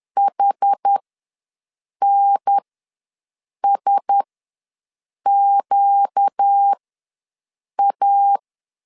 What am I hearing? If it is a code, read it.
Morse code, heard as a keyed tone, about 10 wpm, spaced as for 6 wpm, 800 Hz. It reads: HNSQA
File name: LEZIONI / 6 PKX0 / 6 P ws 10wpm 3751